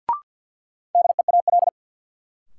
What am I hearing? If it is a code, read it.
Morse code, heard as a keyed tone, about 53 wpm, 700 Hz. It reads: DEAL
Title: Sentences from Top 800 Words - New Word Drill - 50wpm